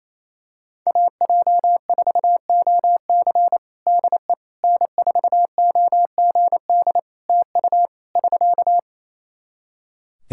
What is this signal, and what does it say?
Morse code, heard as a keyed tone, about 28 wpm, 700 Hz. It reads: AJ4OC DE N4OGD TU <SK>
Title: Practice QSO - Exchange 4 - Ending - 28wpm